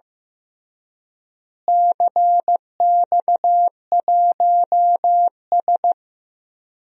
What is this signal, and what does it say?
Morse code, heard as a keyed tone, about 16 wpm, 700 Hz. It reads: CX1S
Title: DXCC-100 Call Signs - 4 Characters - Repeated 3x - 15wpm